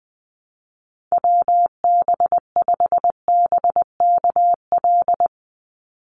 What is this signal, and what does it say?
Morse code, heard as a keyed tone, about 20 wpm, 700 Hz. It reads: WB5BKL
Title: Call Signs - 20wpm